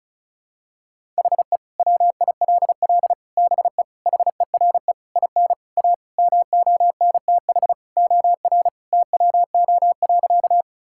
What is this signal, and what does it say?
Morse code, heard as a keyed tone, about 34 wpm, 700 Hz. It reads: HE WILL BE HERE IN A MONTH OR TWO.
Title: Sentences from Top 1000 Words - Review Part 5 - 35wpm